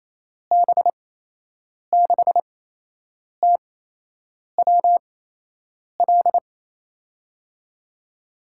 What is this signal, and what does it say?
Morse code, heard as a keyed tone, about 28 wpm, spaced as for 8 wpm, 700 Hz. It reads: B6TWL
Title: USA Call Signs - 6 Characters - 8x Character Spacing - 28wpm